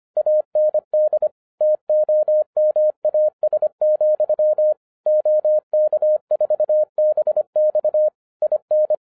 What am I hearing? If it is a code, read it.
Morse code, heard as a keyed tone, about 25 wpm, 610 Hz. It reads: AND TOMAS, OK4BX IN
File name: 2011-07-mp3 / n34 THE 3.11 2 25wpm